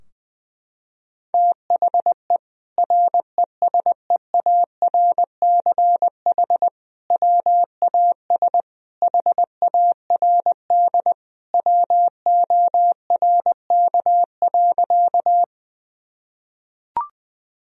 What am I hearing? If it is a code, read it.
Morse code, heard as a keyed tone, about 20 wpm, 700 Hz. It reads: THE RESEARCH WAS HARD WORK.
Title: Sentences from Top 1000 Words - Review Part 4 - 20wpm